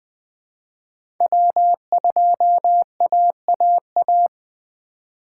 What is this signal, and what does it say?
Morse code, heard as a keyed tone, about 20 wpm, 700 Hz. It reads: W2AAA
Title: USA Call Signs - 5 Characters - 4x Character Spacing - 20wpm